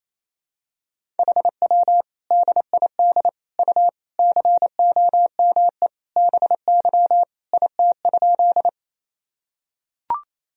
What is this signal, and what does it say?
Morse code, heard as a keyed tone, about 28 wpm, 700 Hz. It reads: HW DID U COME BY IT?
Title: Sentences from Top 100 Words and Abbreviations 28wpm